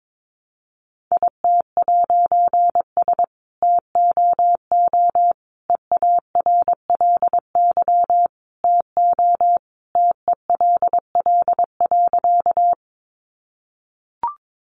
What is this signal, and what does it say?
Morse code, heard as a keyed tone, about 22 wpm, 700 Hz. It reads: IT'S TOO EARLY TO TELL.
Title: A Sound Of Thunder 22wpm